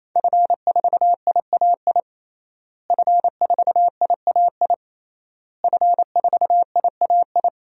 Morse code, heard as a keyed tone, about 28 wpm, 700 Hz. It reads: F4IAI F4IAI F4IAI
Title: DXCC-100 Call Signs - 5 Characters - Repeated 3x - 28wpm